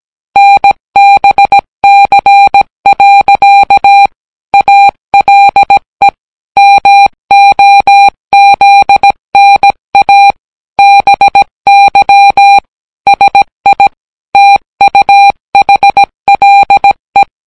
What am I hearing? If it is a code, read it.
Morse code, heard as a keyed tone, about 17 wpm, 800 Hz. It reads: NBC. ALE MOZNA BY SI TUHLE